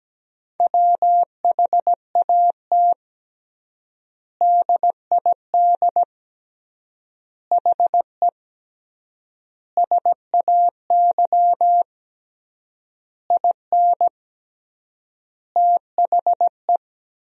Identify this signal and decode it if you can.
Morse code, heard as a keyed tone, about 17 wpm, 700 Hz. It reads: WHAT DID HE SAY IN THE
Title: 3x Word-Spacing - Sentences from Top 300 Words - 17wpm